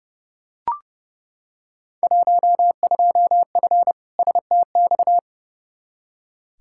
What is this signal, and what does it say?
Morse code, heard as a keyed tone, about 29 wpm, 700 Hz. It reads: 12F STX
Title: Field Day Exchanges Encore - 30wpm